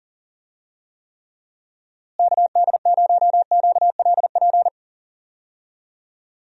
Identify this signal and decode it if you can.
Morse code, heard as a keyed tone, about 40 wpm, 700 Hz. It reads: KD0QLP
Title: USA Call Signs - 6 Characters - Repeated 3x - 40wpm